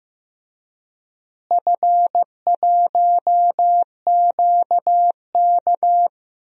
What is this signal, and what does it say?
Morse code, heard as a keyed tone, about 15 wpm, 700 Hz. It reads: F1QK
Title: DXCC-100 Call Signs - 4 Characters - Repeated 2x - 15wpm